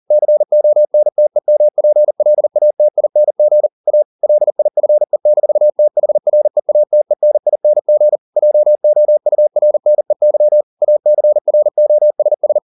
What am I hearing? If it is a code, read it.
Morse code, heard as a keyed tone, about 40 wpm, 600 Hz. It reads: CONTEMPLATING A LIFE-THREATENING JOURNEY ACROSS